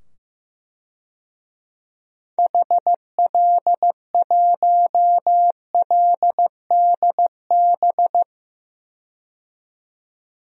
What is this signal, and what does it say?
Morse code, heard as a keyed tone, about 15 wpm, 700 Hz. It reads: HL1LDB
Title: DXCC-100 Call Signs - 6 Characters - 15wpm